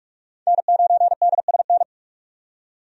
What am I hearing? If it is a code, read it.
Morse code, heard as a keyed tone, about 47 wpm, 700 Hz. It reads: N9DSN
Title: USA Call Signs - 5 Characters - 45wpm